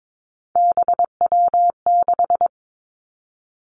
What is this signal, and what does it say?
Morse code, heard as a keyed tone, about 22 wpm, 700 Hz. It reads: BW6
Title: Three Challenging Letter-Number Combinations 22wpm